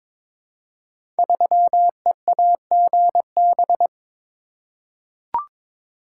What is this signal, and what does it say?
Morse code, heard as a keyed tone, about 22 wpm, 700 Hz. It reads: 3EAGB
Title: Code Groups - 5 Letter:Numbers - 22wpm